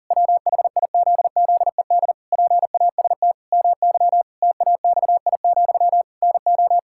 Morse code, heard as a keyed tone, about 40 wpm, 700 Hz. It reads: WHIZZED PAST MY TAXI, NO
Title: Pangram Challenge v2 - 40wpm